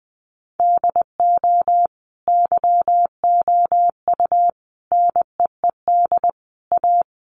Morse code, heard as a keyed tone, about 20 wpm, 700 Hz. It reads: DO YOU NEED A
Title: Sentences from Top 1500 Words 20wpm